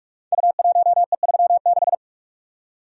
Morse code, heard as a keyed tone, about 44 wpm, 700 Hz. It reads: U1E36
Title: Code Groups - 5 Letter:Numbers - 45wpm